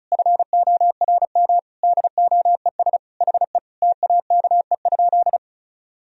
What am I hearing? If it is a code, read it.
Morse code, heard as a keyed tone, about 36 wpm, 700 Hz. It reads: FORM DOES HE TAKE?
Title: Sentences from Top 1000 Words - Review Part 1 - 35wpm